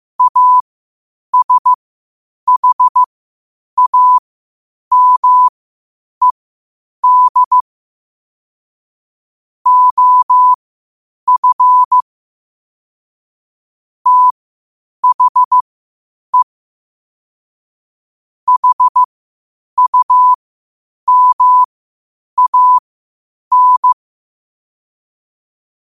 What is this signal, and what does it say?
Morse code, heard as a keyed tone, about 15 wpm, spaced as for 8 wpm, 1000 Hz. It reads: ASHAMED OF THE HUMAN
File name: qotdfiles / 8wpm